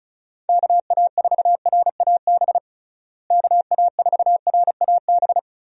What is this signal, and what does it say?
Morse code, heard as a keyed tone, about 36 wpm, 700 Hz. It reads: KA4RAB KA4RAB
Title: USA Call Signs - 6 Characters - Repeated 2x - 35wpm